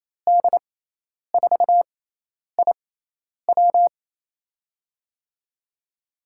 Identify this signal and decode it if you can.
Morse code, heard as a keyed tone, about 28 wpm, spaced as for 10 wpm, 700 Hz. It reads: D4IW
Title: USA Call Signs - 5 Characters - 6x Character Spacing - 28wpm